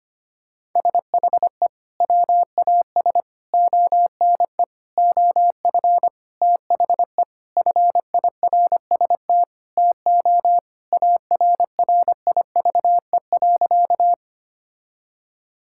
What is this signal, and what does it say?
Morse code, heard as a keyed tone, about 25 wpm, 700 Hz. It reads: SHE WAS ONE OF THE FIRST TO ARRIVE.
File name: practice / 25wpm / Sentences with 4-Word Phrases - 25wpm